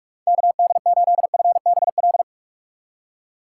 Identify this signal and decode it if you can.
Morse code, heard as a keyed tone, about 46 wpm, 700 Hz. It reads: KD8FBL
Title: USA Call Signs - 6 Characters - 2x Character Spacing - 45wpm